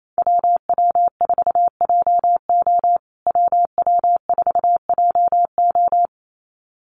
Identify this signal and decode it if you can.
Morse code, heard as a keyed tone, about 28 wpm, 700 Hz. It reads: WW4JO WW4JO
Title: USA Call Signs Repeated 2x 28wpm